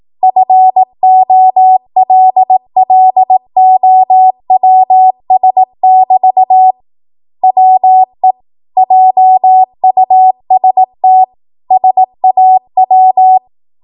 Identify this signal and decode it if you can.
Morse code, heard as a keyed tone, about 18 wpm, 750 Hz. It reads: FOLLOWS <BT> WE JUST SAW
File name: Archive / 18 WPM / 18WPM